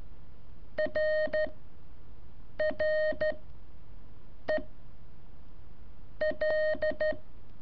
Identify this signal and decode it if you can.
Morse code, heard as a keyed tone, about 13 wpm, 620 Hz. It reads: R R E L
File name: curso / ejercicio 2